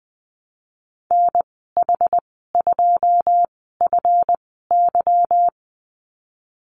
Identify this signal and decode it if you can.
Morse code, heard as a keyed tone, about 19 wpm, spaced as for 14 wpm, 700 Hz. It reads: NH2FY
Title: 179 Lesson 29g - Slash - Call Signs 20wpm